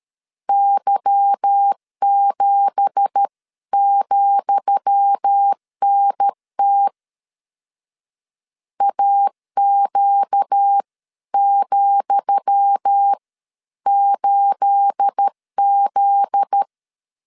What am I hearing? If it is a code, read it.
Morse code, heard as a keyed tone, about 13 wpm, spaced as for 11 wpm, 800 Hz. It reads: Y7,NT AQ,8Z
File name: LEZIONI / 10 SIGNS / 10D COMA 12wpm 2544